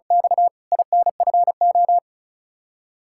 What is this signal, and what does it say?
Morse code, heard as a keyed tone, about 34 wpm, 700 Hz. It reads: X INFO
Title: Practice QSO - Exchange 3 - More Info - 35wpm